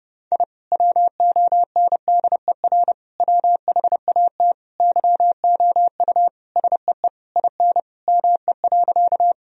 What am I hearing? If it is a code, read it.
Morse code, heard as a keyed tone, about 30 wpm, 700 Hz. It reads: I WONDER WHAT YOU SEE IN ME.
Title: Sentences from Top 1000 Words - Review Part 3 - 30wpm